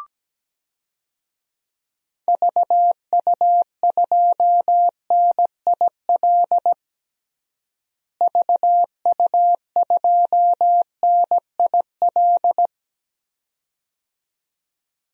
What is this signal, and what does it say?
Morse code, heard as a keyed tone, about 17 wpm, 700 Hz. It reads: VU2NIL VU2NIL
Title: DXCC-100 Call Signs - 6 Characters - Repeated 2x - 17wpm